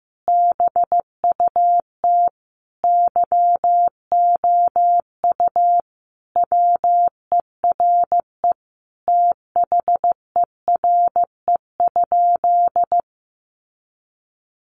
Morse code, heard as a keyed tone, about 15 wpm, 700 Hz. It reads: BUT YOU WERE THERE?
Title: Sentences from Top 500 Words 15wpm